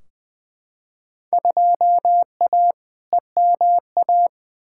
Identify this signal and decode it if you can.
Morse code, heard as a keyed tone, about 20 wpm, 700 Hz. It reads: 2A EMA
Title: Field Day Exchanges Encore - 20wpm